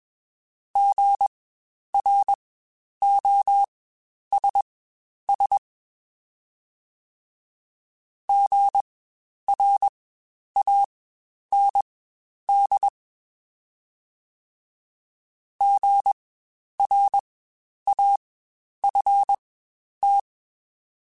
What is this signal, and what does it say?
Morse code, heard as a keyed tone, about 21 wpm, spaced as for 10 wpm, 800 Hz. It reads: GROSS GRAND GRAFT
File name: Spaced Alphas